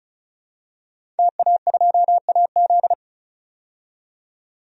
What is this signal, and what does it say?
Morse code, heard as a keyed tone, about 34 wpm, 700 Hz. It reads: TA2AZ
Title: DXCC-100 Call Signs - 5 Characters - 35wpm